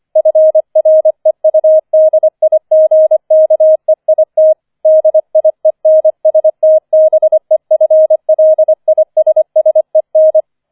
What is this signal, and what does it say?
Morse code, heard as a keyed tone, about 25 wpm, 630 Hz. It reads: FREUDIGKEIT DIENSTBEFLISSEN
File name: 005 Fünfter Streich 25wpm